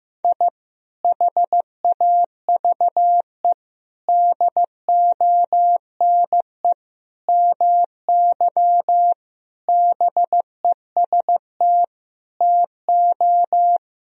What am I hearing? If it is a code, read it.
Morse code, heard as a keyed tone, about 15 wpm, 700 Hz. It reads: I HAVE DONE MY BEST TO
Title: Sentences from Top 1000 Words - Review Part 4 - 15wpm